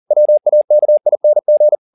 Morse code, heard as a keyed tone, about 40 wpm, 600 Hz. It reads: WAKING